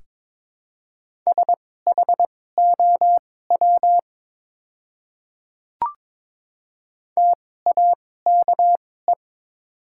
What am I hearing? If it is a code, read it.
Morse code, heard as a keyed tone, about 22 wpm, spaced as for 16 wpm, 700 Hz. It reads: SHOW TAKE
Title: 2x Character Spacing - Top 200 Words - 22wpm